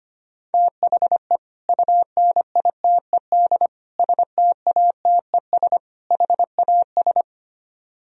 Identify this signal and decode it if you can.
Morse code, heard as a keyed tone, about 32 wpm, 700 Hz. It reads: THE UNITED STATES HAS